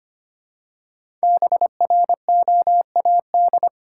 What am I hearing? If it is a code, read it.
Morse code, heard as a keyed tone, about 25 wpm, 700 Hz. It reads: BROAD
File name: 5 Letter Words - First 3 Letters Given - 25wpm